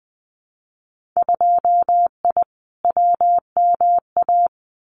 Morse code, heard as a keyed tone, about 20 wpm, 700 Hz. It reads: 2I WMA
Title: Winter Field Day Exchanges Encore 20wpm